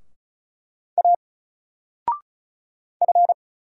Morse code, heard as a keyed tone, about 36 wpm, 700 Hz. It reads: A F
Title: Single Letter-Number 35wpm